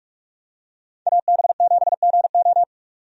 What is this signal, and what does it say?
Morse code, heard as a keyed tone, about 46 wpm, 700 Hz. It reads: AB7GO
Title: USA Call Signs - 5 Characters - Repeated 2x - 45wpm